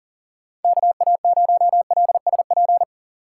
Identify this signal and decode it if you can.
Morse code, heard as a keyed tone, about 40 wpm, 700 Hz. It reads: KA0LSP
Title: USA Call Signs - 6 Characters - 6x Character Spacing - 40wpm